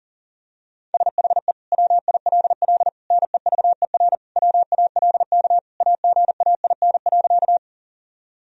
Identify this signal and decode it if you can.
Morse code, heard as a keyed tone, about 37 wpm, 700 Hz. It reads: SHE WILL NEVER WALK AGAIN.
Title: Sentences from Top 1000 Words - Review Part 1 - 40wpm